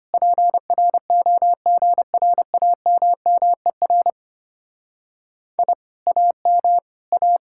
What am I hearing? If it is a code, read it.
Morse code, heard as a keyed tone, about 32 wpm, 700 Hz. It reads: PROGRAMMER I AM A